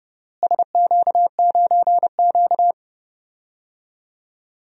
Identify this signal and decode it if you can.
Morse code, heard as a keyed tone, about 30 wpm, 700 Hz. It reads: SQ9Q
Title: DXCC-100 Call Signs - 4 Characters - Repeated 3x - 30wpm